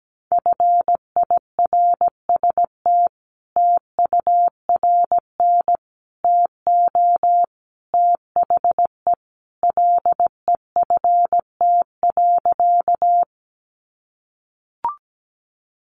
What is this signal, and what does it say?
Morse code, heard as a keyed tone, about 17 wpm, 700 Hz. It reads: FIRST TURN TO THE LEFT.
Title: Sentences from Top 300 Words 17wpm